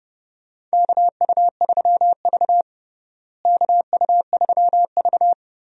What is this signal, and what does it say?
Morse code, heard as a keyed tone, about 31 wpm, 700 Hz. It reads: KU3V KU3V
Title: USA Call Signs - 4 Characters - Repeated 2x - 30wpm